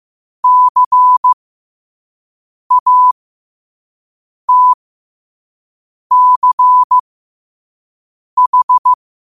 Morse code, heard as a keyed tone, about 15 wpm, spaced as for 5 wpm, 1000 Hz. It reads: CATCH
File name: qotdfiles / 5wpm